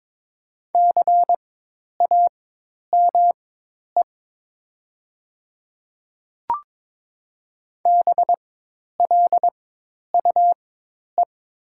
Morse code, heard as a keyed tone, about 23 wpm, spaced as for 10 wpm, 700 Hz. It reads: CAME BLUE
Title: 4x Character Spacing - Top 500 Words - 22wpm